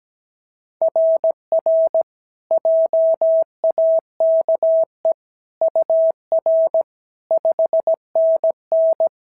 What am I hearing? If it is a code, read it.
Morse code, heard as a keyed tone, about 17 wpm, 660 Hz. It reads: RR JAKE UR 5NN
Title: SOTA QSO Practice 17wpm